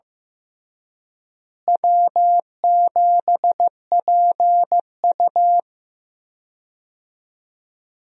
Morse code, heard as a keyed tone, about 15 wpm, 700 Hz. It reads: W7PU
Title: USA Call Signs - 4 Characters - Repeated 2x - 15wpm